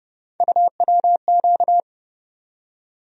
Morse code, heard as a keyed tone, about 30 wpm, 700 Hz. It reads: UWQ
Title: USA Call Sign Postfixes - 3 Letters 30wpm